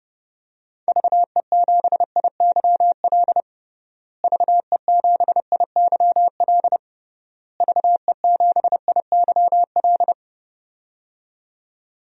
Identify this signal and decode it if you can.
Morse code, heard as a keyed tone, about 30 wpm, 700 Hz. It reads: VE7IYL VE7IYL VE7IYL
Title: DXCC-100 Call Signs - 6 Characters - Repeated 3x - 30wpm